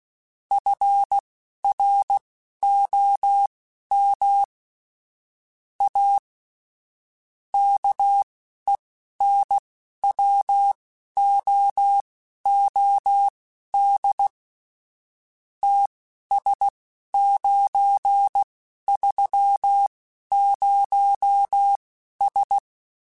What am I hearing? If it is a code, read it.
Morse code, heard as a keyed tone, about 16 wpm, spaced as for 11 wpm, 800 Hz. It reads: FROM A KENWOOD TS930S